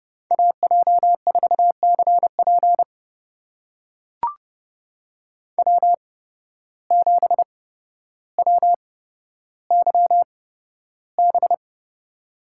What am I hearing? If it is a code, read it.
Morse code, heard as a keyed tone, about 31 wpm, spaced as for 13 wpm, 700 Hz. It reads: AJ4CP W7WYB
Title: USA Call Signs - 5 Characters - 8x Character Spacing - 30wpm